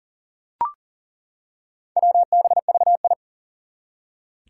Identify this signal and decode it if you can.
Morse code, heard as a keyed tone, about 39 wpm, 700 Hz. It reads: WBVI
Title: Four Challenging Letter-Number Combinations 40wpm